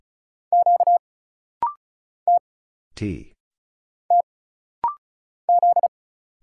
Morse code, heard as a keyed tone, about 34 wpm, 700 Hz. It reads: Q T T Z
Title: Single Letter-Number Alt - Rapid-Fire 35wpm